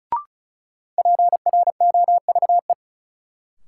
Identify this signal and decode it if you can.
Morse code, heard as a keyed tone, about 34 wpm, 700 Hz. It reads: PROVE
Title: Sentences from Top 1600 Words - New Word Drill - 35wpm